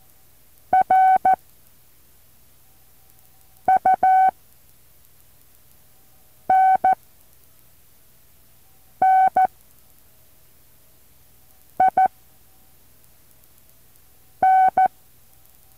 Morse code, heard as a keyed tone, about 14 wpm, spaced as for 4 wpm, 760 Hz. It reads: RUNNIN
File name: intl morse code2